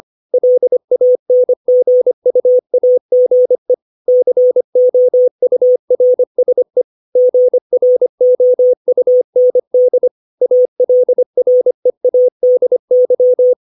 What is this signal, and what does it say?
Morse code, heard as a keyed tone, about 25 wpm, 500 Hz. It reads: LANGUAGE COURSE GROUND ALREADY
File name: 632 words / 632 Words 25WPM